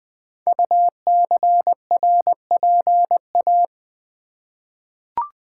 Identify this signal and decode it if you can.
Morse code, heard as a keyed tone, about 20 wpm, 700 Hz. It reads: UCRPA
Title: Code Groups - 5 Letter:Numbers - 20wpm